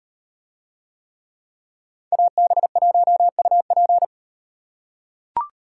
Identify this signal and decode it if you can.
Morse code, heard as a keyed tone, about 38 wpm, 700 Hz. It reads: AB1UP